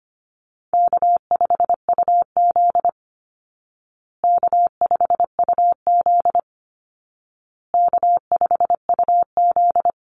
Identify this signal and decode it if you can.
Morse code, heard as a keyed tone, about 25 wpm, 700 Hz. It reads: K5UZ K5UZ K5UZ
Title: Call Signs Repeated 2x - 25wpm